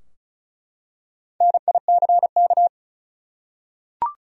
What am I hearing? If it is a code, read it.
Morse code, heard as a keyed tone, about 36 wpm, 700 Hz. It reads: NICK
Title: USA Common Names - Max 4 Letters - 35wpm